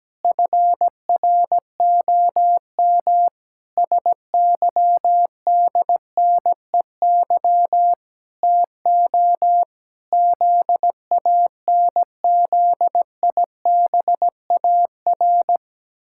Morse code, heard as a keyed tone, about 17 wpm, 700 Hz. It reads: FROM SYDNEY TO ZANZIBAR
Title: Pangram Challenge v2 - 17wpm